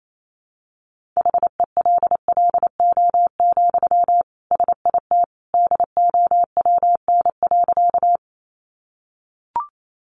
Morse code, heard as a keyed tone, about 28 wpm, 700 Hz. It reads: HELLO, SIT DOWN.
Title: Sentences from Top 1100 Words 28wpm